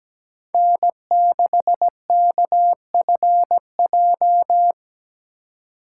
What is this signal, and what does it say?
Morse code, heard as a keyed tone, about 17 wpm, 700 Hz. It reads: N6KFJ
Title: USA Call Signs - 5 Characters - 3x Character Spacing - 17wpm